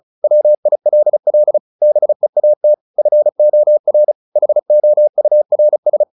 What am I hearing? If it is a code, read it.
Morse code, heard as a keyed tone, about 34 wpm, 600 Hz. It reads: WILL BEAT FOR HOURS
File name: the old man and the sea / the old man and the sea 1 35wpm